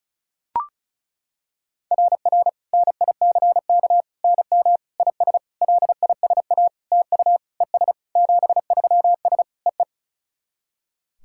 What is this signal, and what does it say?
Morse code, heard as a keyed tone, about 36 wpm, 690 Hz. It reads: RR NICK NM IS LISA TU ES 73S EE
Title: SOTA QSO Practice 35wpm